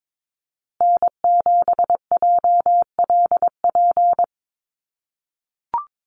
Morse code, heard as a keyed tone, about 22 wpm, 700 Hz. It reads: N7JLP
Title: USA Call Signs Repeated 2x 22wpm